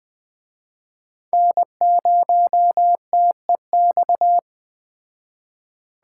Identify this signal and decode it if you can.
Morse code, heard as a keyed tone, about 20 wpm, 700 Hz. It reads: N0TEX